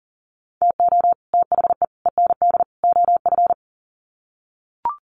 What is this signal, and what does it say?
Morse code, heard as a keyed tone, about 40 wpm, 700 Hz. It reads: TO THE END OF
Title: Sets of 4 Words 40wpm